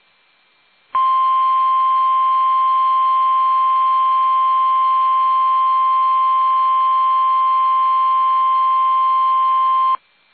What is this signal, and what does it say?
Emergency Alert System (SAME) tones (the 1050 Hz attention tone). It is heard from 0.9 to 10.0 s.
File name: eas / KUSO / KUSO